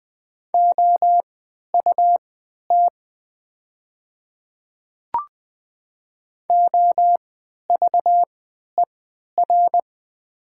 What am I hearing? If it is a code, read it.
Morse code, heard as a keyed tone, about 20 wpm, spaced as for 11 wpm, 700 Hz. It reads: OUT OVER
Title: 3x Character Spacing - Top 100 Words - 20wpm